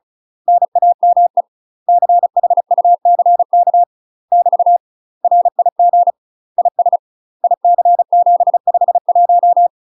Morse code, heard as a keyed tone, about 34 wpm, 700 Hz. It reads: NAME CHUCK <BT> RIG IS IC751
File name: UK MASTER QSOs TRAINING/35WPM